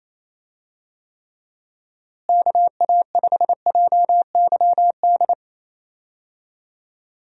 Morse code, heard as a keyed tone, about 28 wpm, 700 Hz. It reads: KA5JYD
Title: USA Call Signs - 6 Characters - Repeated 2x - 28wpm